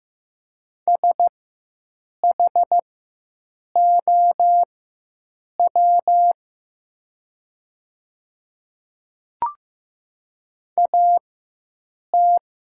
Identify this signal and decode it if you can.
Morse code, heard as a keyed tone, about 15 wpm, spaced as for 7 wpm, 700 Hz. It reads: SHOW AT
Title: 4x Character Spacing - Top 200 Words - 15wpm